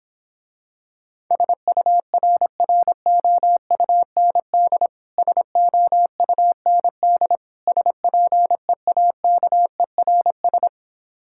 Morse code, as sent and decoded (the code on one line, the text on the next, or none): ... ..- .-. .-. --- ..- -. -.. / ... --- ..- -. -.. / ... .--. .- -.- . .-. ...
SURROUND SOUND SPEAKERS